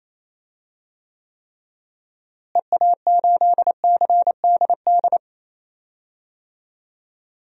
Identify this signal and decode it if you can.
Morse code, heard as a keyed tone, about 28 wpm, 700 Hz. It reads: EA8CDD
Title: DXCC-100 Call Signs - 6 Characters - Repeated 3x - 28wpm